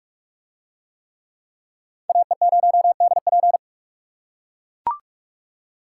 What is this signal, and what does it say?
Morse code, heard as a keyed tone, about 44 wpm, 700 Hz. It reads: AE0DP